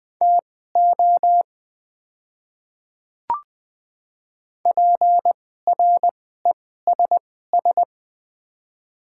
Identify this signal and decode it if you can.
Morse code, heard as a keyed tone, about 20 wpm, spaced as for 14 wpm, 700 Hz. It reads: TO PRESS
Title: 2x Character Spacing - Top 300 Words - 20wpm